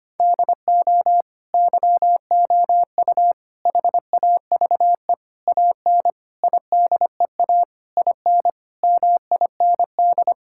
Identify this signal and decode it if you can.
Morse code, heard as a keyed tone, about 25 wpm, 700 Hz. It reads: DO YOU HAVE AN IDEA IN MIND
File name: practice / 25wpm / Sentences with 9-Word Phrases - 25wpm